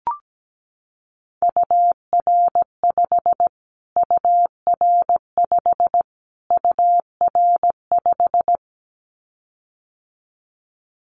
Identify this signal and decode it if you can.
Morse code, heard as a keyed tone, about 17 wpm, 700 Hz. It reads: UR5 UR5 UR5
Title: Callsign Prefixes for Top 25 DXCC - Repeated 3x - 17wpm